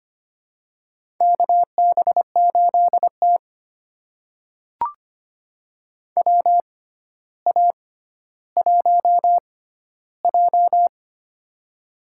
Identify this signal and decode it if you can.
Morse code, heard as a keyed tone, about 25 wpm, spaced as for 13 wpm, 700 Hz. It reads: KB8T WA1J
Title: USA Call Signs - 4 Characters - 6x Character Spacing - 25wpm